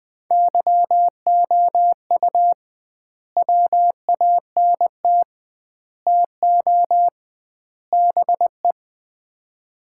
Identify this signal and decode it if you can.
Morse code, heard as a keyed tone, about 21 wpm, 700 Hz. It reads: YOU WANT TO BE
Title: 2x Word-Spacing - Sets of 4 Words - 20wpm